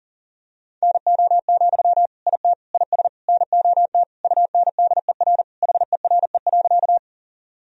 Morse code, heard as a keyed tone, about 40 wpm, 700 Hz. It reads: NO, IT IS NOT UNDER HERE.
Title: Sentences from Top 1000 Words - Review Part 1 - 40wpm